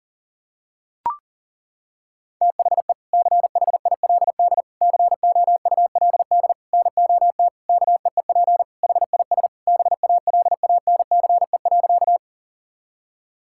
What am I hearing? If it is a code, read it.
Morse code, heard as a keyed tone, about 40 wpm, 700 Hz. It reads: THE CHILD COULD NOT KEEP HIS BALANCE.
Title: Sentences from Top 1000 Words - Review Part 6 - 40wpm